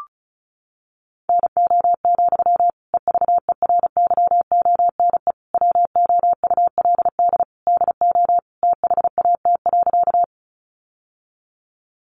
Morse code, heard as a keyed tone, about 34 wpm, 700 Hz. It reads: NO, EVERYONE WOULD DO THAT.
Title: Sentences from Top 800 Words 35wpm